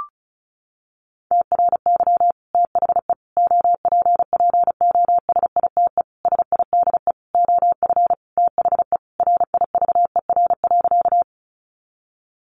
Morse code, heard as a keyed tone, about 34 wpm, 700 Hz. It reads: TRY THE OPPOSITE SIDE OF THE RIVER.
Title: Sentences from Top 1300 Words 35wpm